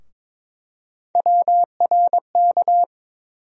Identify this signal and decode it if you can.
Morse code, heard as a keyed tone, about 22 wpm, 700 Hz. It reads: WRK
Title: USA Call Sign Postfixes - 3 Letters 22wpm